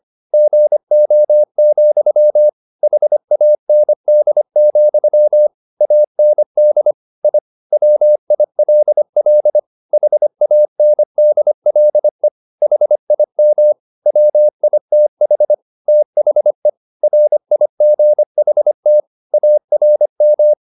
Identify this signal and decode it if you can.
Morse code, heard as a keyed tone, about 25 wpm, 600 Hz. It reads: GO, HAND, AND I WILL HANDLE HIM WITH THE RIGHT ARM
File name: the old man and the sea part 1 25wpm